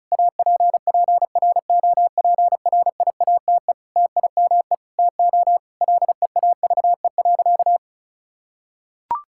Morse code, heard as a keyed tone, about 36 wpm, 700 Hz. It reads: APPROPRIATE TIME TO LEAVE.
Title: Sentences from Top 1000 Words - Review Part 5 - 35wpm